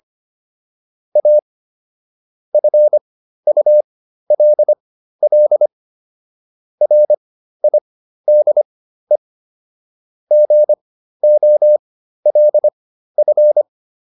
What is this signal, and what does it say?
Morse code, heard as a keyed tone, about 25 wpm, spaced as for 13 wpm, 600 Hz. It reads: A FULL RIDE GOLF